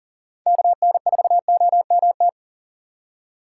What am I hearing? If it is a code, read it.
Morse code, heard as a keyed tone, about 40 wpm, 700 Hz. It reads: KN4OMT
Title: USA Call Signs - 6 Characters - 3x Character Spacing - 40wpm